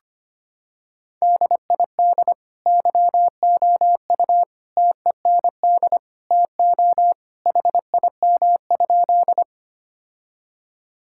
Morse code, heard as a keyed tone, about 25 wpm, 700 Hz. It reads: DID YOU TEND TO HIM?
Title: Sentences from Top 1000 Words - Review Part 5 - 25wpm